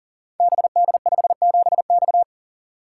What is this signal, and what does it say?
Morse code, heard as a keyed tone, about 39 wpm, 700 Hz. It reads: BD57X
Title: Code Groups - 5 Letter:Numbers - 40wpm